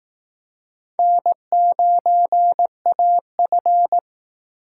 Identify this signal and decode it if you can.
Morse code, heard as a keyed tone, about 18 wpm, 700 Hz. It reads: N9AF
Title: Speed Racing - USA 1x2 Call Signs 15wpm